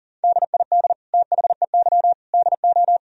Morse code, heard as a keyed tone, about 40 wpm, 700 Hz. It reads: DID THEY DO
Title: Sentences from Top 1000 Words - Review Part 4 - 40wpm